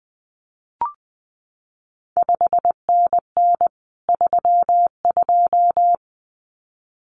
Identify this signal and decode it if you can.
Morse code, heard as a keyed tone, about 20 wpm, 700 Hz. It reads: 5NN 32
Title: CQ WW DX Contest Exchange Encore - 20wpm